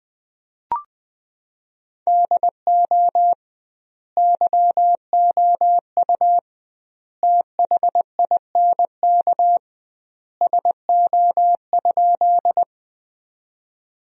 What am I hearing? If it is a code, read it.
Morse code, heard as a keyed tone, about 20 wpm, 700 Hz. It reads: DO YOU THINK SO?
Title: 2x Word-Spacing - Sentences from Top 200 Words - 20wpm